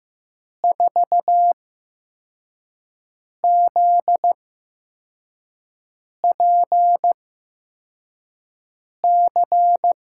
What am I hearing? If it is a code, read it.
Morse code, heard as a keyed tone, about 15 wpm, spaced as for 4 wpm, 700 Hz. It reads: 4ZPC
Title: USA Call Signs - 6 Characters - 8x Character Spacing - 15wpm